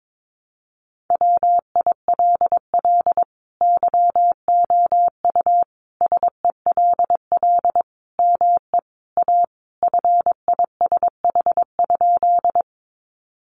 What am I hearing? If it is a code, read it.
Morse code, heard as a keyed tone, about 22 wpm, 700 Hz. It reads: WILL YOU SELL ME A FISH?
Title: Sentences from Top 1100 Words 22wpm